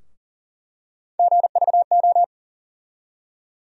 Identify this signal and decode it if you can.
Morse code, heard as a keyed tone, about 39 wpm, 700 Hz. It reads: GVO
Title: USA Call Sign Postfixes - 3 Letters 40wpm